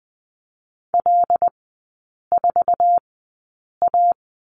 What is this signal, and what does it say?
Morse code, heard as a keyed tone, about 20 wpm, spaced as for 8 wpm, 700 Hz. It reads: L4A
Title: 081 Lesson 14e - Number 2 - Three Characters 20wpm